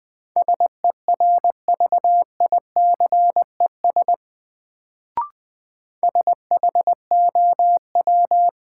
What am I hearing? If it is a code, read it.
Morse code, heard as a keyed tone, about 20 wpm, 700 Hz. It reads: SERVICES SHOW
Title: Sets of 5 Words - New Word Drill - 20wpm